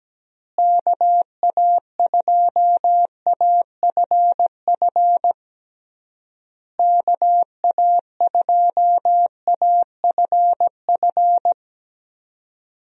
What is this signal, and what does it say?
Morse code, heard as a keyed tone, about 17 wpm, 700 Hz. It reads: KA2AFF KA2AFF
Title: USA Call Signs - 6 Characters - Repeated 3x - 17wpm